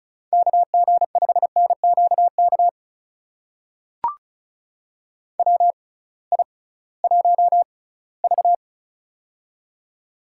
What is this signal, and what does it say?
Morse code, heard as a keyed tone, about 36 wpm, spaced as for 20 wpm, 700 Hz. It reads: KG5NQK WI1V
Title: USA Call Signs - 6x Character Spacing - 35wpm